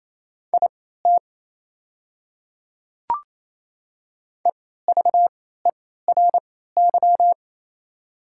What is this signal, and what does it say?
Morse code, heard as a keyed tone, about 27 wpm, spaced as for 16 wpm, 700 Hz. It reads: IT EVERY
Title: 3x Character Spacing - Top 200 Words - 28wpm